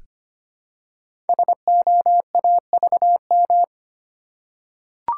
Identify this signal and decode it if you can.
Morse code, heard as a keyed tone, about 25 wpm, 700 Hz. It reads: SOAVM
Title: Code Groups - 5 Letters - 25wpm